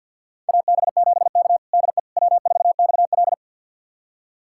Morse code, heard as a keyed tone, about 48 wpm, 700 Hz. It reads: AB7K DE W4XL
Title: USA Callsign DE Callsign - 4 Characters - 50wpm